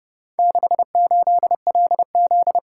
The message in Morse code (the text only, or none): 68LZ